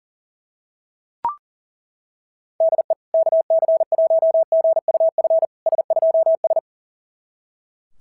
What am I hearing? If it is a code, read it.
Morse code, heard as a keyed tone, about 38 wpm, 660 Hz. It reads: DE KC1GUF S2S
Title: SOTA QSO Practice 40wpm